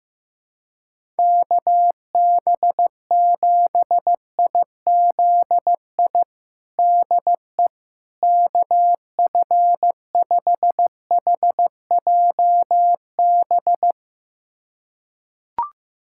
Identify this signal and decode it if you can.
Morse code, heard as a keyed tone, about 15 wpm, 700 Hz. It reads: KB7IZI DE KF5HJB
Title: USA Callsign DE Callsign - 6 Characters - 15wpm